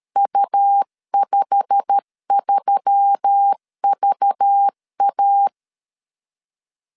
Morse code, heard as a keyed tone, about 13 wpm, 800 Hz. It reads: U53VA
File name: LEZIONI / 8 YJ45 / 8 YJ45 12wpm 2151